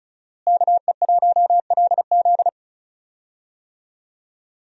Morse code, heard as a keyed tone, about 36 wpm, 700 Hz. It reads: KE1LZ
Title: USA Call Signs - 5 Characters - Repeated 2x - 35wpm